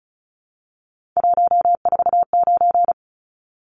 Morse code, heard as a keyed tone, about 34 wpm, 700 Hz. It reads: 149
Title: Numbers - 3-Digit 35wpm